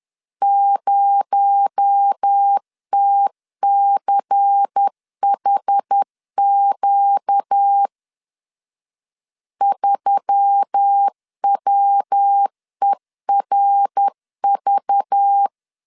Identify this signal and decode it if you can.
Morse code, heard as a keyed tone, about 10 wpm, 800 Hz. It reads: 0TCHQ 3WERV
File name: LEZIONI / 6 PKX0 / 6 PKX0 10wpm 2707